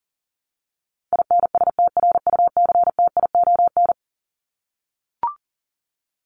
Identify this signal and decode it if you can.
Morse code, heard as a keyed tone, about 40 wpm, 700 Hz. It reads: INSTRUCTION
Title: Top 1751-2000 Words 40wpm